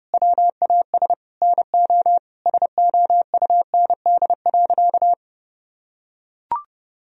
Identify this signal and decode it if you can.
Morse code, heard as a keyed tone, about 29 wpm, 700 Hz. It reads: WAS NO SOUND.
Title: Sentences from Top 1000 Words - Review Part 2 - 30wpm